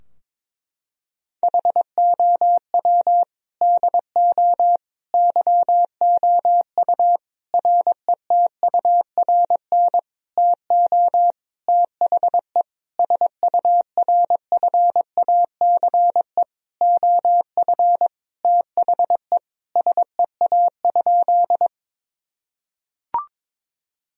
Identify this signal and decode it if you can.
Morse code, heard as a keyed tone, about 22 wpm, 700 Hz. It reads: HOW DO YOU RETURN TO THE SURFACE OF THE SEA?
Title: Sentences from Top 1000 Words - Review Part 3 - 22wpm